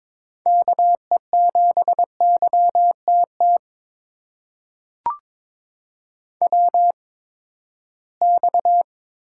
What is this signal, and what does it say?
Morse code, heard as a keyed tone, about 22 wpm, spaced as for 15 wpm, 700 Hz. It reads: KE7YTT WX
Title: USA Call Signs - 6 Characters - 8x Character Spacing - 22wpm